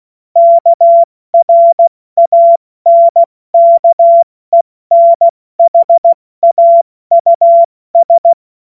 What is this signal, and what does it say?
Morse code, heard as a keyed tone, about 16 wpm, 680 Hz. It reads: KRANKENHAUS
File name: Morsekurs / Witze / Witz 20. Dezember 2022 70 F80